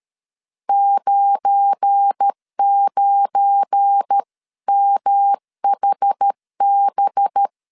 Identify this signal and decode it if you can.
Morse code, heard as a keyed tone, about 13 wpm, 800 Hz. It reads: 99MHB
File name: LEZIONI / 7 GF9Z / 7 GF9 12wpm 2319